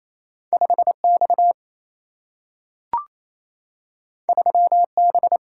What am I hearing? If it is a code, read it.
Morse code, heard as a keyed tone, about 28 wpm, 700 Hz. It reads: 5X 3B
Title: Single-Letter Single-Number 28wpm